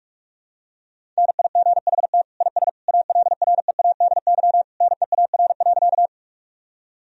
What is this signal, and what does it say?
Morse code, heard as a keyed tone, about 42 wpm, 700 Hz. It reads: NIGHT IS ALREADY NEAR.
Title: Sentences from Top 1000 Words - Review Part 1 - 45wpm